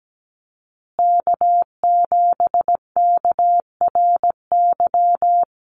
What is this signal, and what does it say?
Morse code, heard as a keyed tone, about 17 wpm, 700 Hz. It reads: K7KRY
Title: USA Call Signs Repeated 2x 17wpm